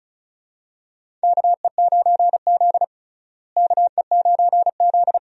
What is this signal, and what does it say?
Morse code, heard as a keyed tone, about 34 wpm, 700 Hz. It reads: KE9Z KE9Z
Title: USA Call Signs - 4 Characters - Repeated 2x - 35wpm